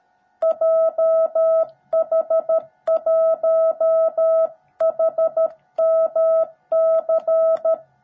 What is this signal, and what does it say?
Morse code, heard as a keyed tone, about 13 wpm, 660 Hz. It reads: JH1HMC
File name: jh8cbh / 2021cw / cw17b